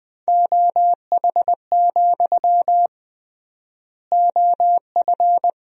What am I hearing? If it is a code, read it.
Morse code, heard as a keyed tone, about 20 wpm, 700 Hz. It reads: OH, OF
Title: 3x Word-Spacing - Sentences from Top 500 Words - 20wpm